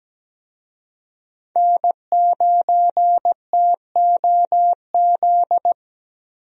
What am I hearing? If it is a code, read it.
Morse code, heard as a keyed tone, about 17 wpm, 700 Hz. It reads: N9TOZ
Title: USA Call Signs - 5 Characters - Repeated 3x - 17wpm